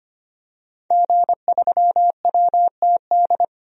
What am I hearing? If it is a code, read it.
Morse code, heard as a keyed tone, about 25 wpm, 700 Hz. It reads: G3WTD
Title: DXCC-100 Call Signs - 5 Characters - 25wpm